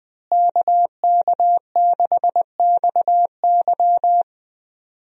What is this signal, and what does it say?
Morse code, heard as a keyed tone, about 21 wpm, 700 Hz. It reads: KK6XY
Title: USA Call Signs - 5 Characters - Repeated 2x - 20wpm